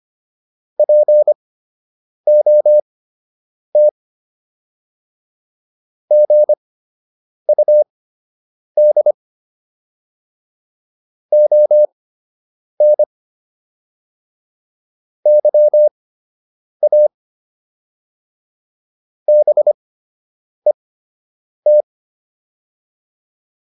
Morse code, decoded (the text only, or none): POT GUD ON YA BET